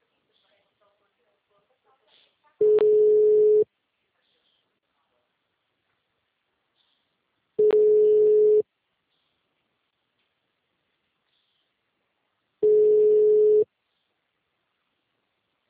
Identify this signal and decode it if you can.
Ringback tone (Europe, 425 Hz).
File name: out- -2001- - - .46521